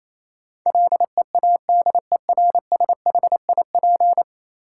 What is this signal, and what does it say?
Morse code, heard as a keyed tone, about 28 wpm, 700 Hz. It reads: LEADERSHIP
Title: Sentences from Top 1600 Words - New Word Drill - 28wpm